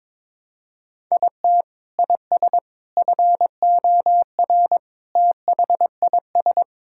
Morse code, heard as a keyed tone, about 21 wpm, 700 Hz. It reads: IT IS FOR THIS
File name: practice / 22wpm / Sets of 6 Words - 22wpm